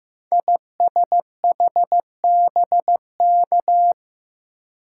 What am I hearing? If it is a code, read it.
Morse code, heard as a keyed tone, about 15 wpm, 700 Hz. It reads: ISHBK
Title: Code Groups - 5 Letters - 15wpm